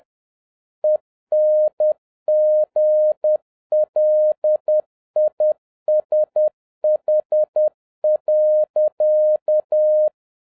Morse code, heard as a keyed tone, about 10 wpm, 610 Hz. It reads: ENGLISH.
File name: 2011-07-mp3 / n21 GREETINGS FROM SINGAPORE 10wpm